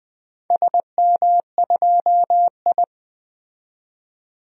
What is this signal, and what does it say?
Morse code, heard as a keyed tone, about 20 wpm, 700 Hz. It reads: SM2I
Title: DXCC-100 Call Signs - 4 Characters - 20wpm